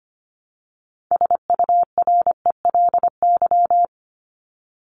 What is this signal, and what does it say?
Morse code, heard as a keyed tone, about 25 wpm, 700 Hz. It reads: SURELY
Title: LY Suffix - Top 100 Words with Base Words 25wpm